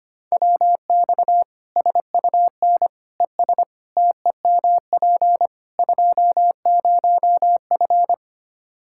Morse code, heard as a keyed tone, about 25 wpm, 700 Hz. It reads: WX SUN ES TEMP 20F
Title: Practice QSO - Exchange 2 - Info - 25wpm